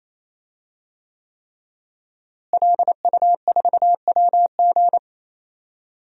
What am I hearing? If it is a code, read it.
Morse code, heard as a keyed tone, about 28 wpm, 700 Hz. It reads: LU4WG
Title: DXCC-100 Call Signs - 5 Characters - Repeated 3x - 28wpm